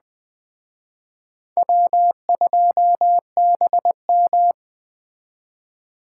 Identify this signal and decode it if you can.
Morse code, heard as a keyed tone, about 20 wpm, 700 Hz. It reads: W2BM
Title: Speed Racing - USA 1x2 Call Signs 17wpm